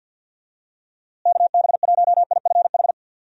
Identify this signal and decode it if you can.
Morse code, heard as a keyed tone, about 49 wpm, 700 Hz. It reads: KB1IFH
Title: USA Call Signs - 6 Characters - Repeated 3x - 50wpm